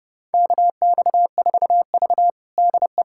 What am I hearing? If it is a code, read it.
Morse code, heard as a keyed tone, about 32 wpm, 700 Hz. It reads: KX4V DE
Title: USA Callsign DE Callsign - 4 Characters - 30wpm